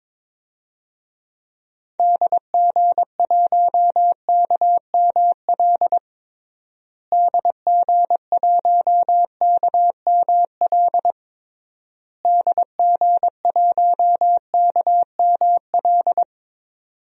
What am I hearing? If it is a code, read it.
Morse code, heard as a keyed tone, about 22 wpm, 700 Hz. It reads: DG1KML DG1KML DG1KML
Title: DXCC-100 Call Signs - 6 Characters - Repeated 3x - 22wpm